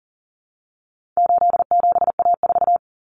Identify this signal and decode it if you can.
Morse code, heard as a keyed tone, about 40 wpm, 700 Hz. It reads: 87A4
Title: Common Cut Numbers - 4-Digit 40wpm